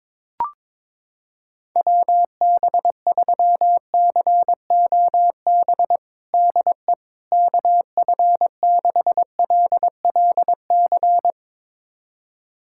Morse code, heard as a keyed tone, about 22 wpm, 700 Hz. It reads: WB3COB DE KF6LLC
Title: USA Callsign DE Callsign - 6 Characters - 22wpm